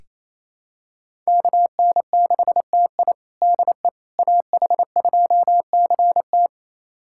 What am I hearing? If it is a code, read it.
Morse code, heard as a keyed tone, about 28 wpm, 700 Hz. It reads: KN6TI DE AH2CT
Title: USA Callsign DE Callsign - 5 Characters - 28wpm